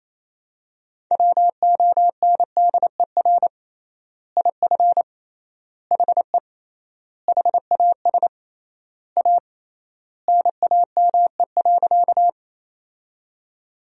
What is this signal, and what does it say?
Morse code, heard as a keyed tone, about 28 wpm, 700 Hz. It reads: WONDER IF HE HAS A NAME.
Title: 3x Word-Spacing - Sentences from Top 500 Words - 28wpm